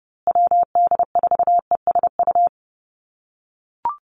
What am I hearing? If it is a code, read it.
Morse code, heard as a keyed tone, about 30 wpm, 700 Hz. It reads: WD4ESU
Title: USA Call Signs Repeated 3x 30wpm